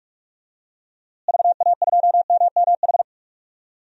Morse code, heard as a keyed tone, about 42 wpm, 700 Hz. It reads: VA2MMH
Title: DXCC-100 Call Signs - 6 Characters - Repeated 2x - 45wpm